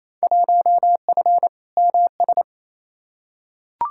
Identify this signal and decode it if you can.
Morse code, heard as a keyed tone, about 28 wpm, 700 Hz. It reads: 1F MS
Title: Field Day Exchanges Encore - 28wpm